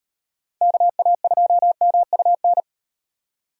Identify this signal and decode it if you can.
Morse code, heard as a keyed tone, about 38 wpm, 700 Hz. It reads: KA2MUN